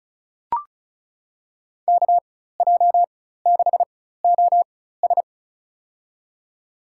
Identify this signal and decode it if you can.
Morse code, heard as a keyed tone, about 36 wpm, spaced as for 16 wpm, 700 Hz. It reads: KJ6OS
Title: USA Call Signs - 5 Characters - 4x Character Spacing - 35wpm